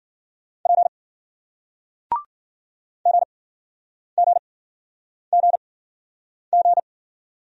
Morse code, heard as a keyed tone, about 56 wpm, spaced as for 9 wpm, 700 Hz. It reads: P GGGG